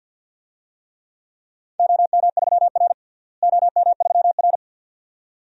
Morse code, heard as a keyed tone, about 52 wpm, 700 Hz. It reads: OM3R OM3R
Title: DXCC-100 Call Signs - 4 Characters - Repeated 2x - 50wpm